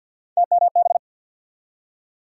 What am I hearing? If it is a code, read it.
Morse code, heard as a keyed tone, about 47 wpm, 700 Hz. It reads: TMB
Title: USA Call Sign Postfixes - 3 Letters 50wpm